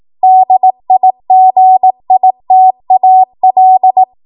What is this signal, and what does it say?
Morse code, heard as a keyed tone, about 18 wpm, 750 Hz. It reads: DIGITAL
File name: Morse / 18WPM